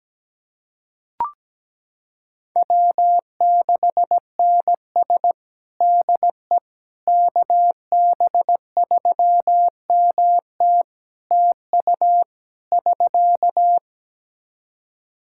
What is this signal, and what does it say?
Morse code, heard as a keyed tone, about 17 wpm, 700 Hz. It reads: W6NS DE KB3MT TU <SK>
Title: Practice QSO - Exchange 4 - Ending - 17wpm